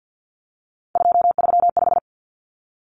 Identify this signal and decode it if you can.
Morse code, heard as a keyed tone, about 50 wpm, 700 Hz. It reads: 235
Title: Common Cut Numbers - 3-Digit 50wpm